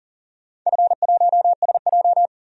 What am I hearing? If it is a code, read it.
Morse code, heard as a keyed tone, about 42 wpm, 700 Hz. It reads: F1SJ
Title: DXCC-100 Call Signs - 4 Characters - 40wpm